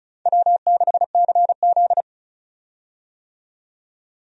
Morse code, heard as a keyed tone, about 36 wpm, 700 Hz. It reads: W6CZ
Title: USA Call Signs - 4 Characters - 35wpm